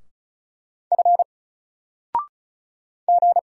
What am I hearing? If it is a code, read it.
Morse code, heard as a keyed tone, about 35 wpm, 700 Hz. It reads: F G
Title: Single Letter-Number 35wpm